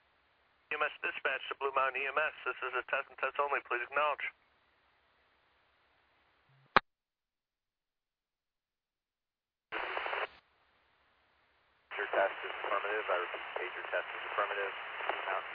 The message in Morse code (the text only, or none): none